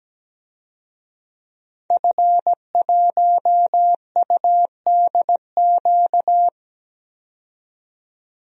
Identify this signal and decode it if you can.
Morse code, heard as a keyed tone, about 17 wpm, 700 Hz. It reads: F1UDQ
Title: DXCC-100 Call Signs - 5 Characters - 17wpm